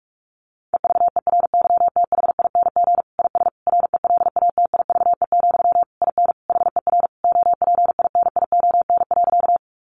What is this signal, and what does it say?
Morse code, heard as a keyed tone, about 44 wpm, 700 Hz. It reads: EVERYTHING IS RELATIVE, IN HER OPINION.
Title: Sentences from Top 1400 Words 45wpm